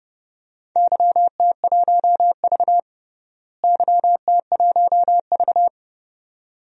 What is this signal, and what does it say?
Morse code, heard as a keyed tone, about 32 wpm, 700 Hz. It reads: YT1V YT1V
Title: DXCC-100 Call Signs - 4 Characters - Repeated 2x - 30wpm